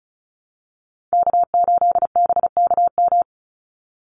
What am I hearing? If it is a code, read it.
Morse code, heard as a keyed tone, about 34 wpm, 700 Hz. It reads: K8BKM
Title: Call Signs Repeated 3x - 35wpm